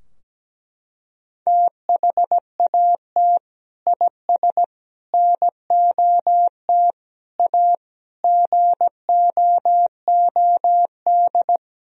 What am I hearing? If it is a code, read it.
Morse code, heard as a keyed tone, about 17 wpm, 700 Hz. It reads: THAT IS NOT A GOOD
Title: Sentences from Top 1000 Words - Review Part 3 - 17wpm